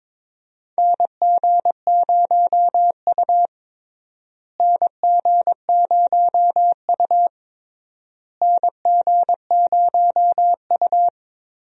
Morse code, heard as a keyed tone, about 22 wpm, 700 Hz. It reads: NG0U NG0U NG0U
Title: USA Call Signs - 4 Characters - Repeated 3x - 22wpm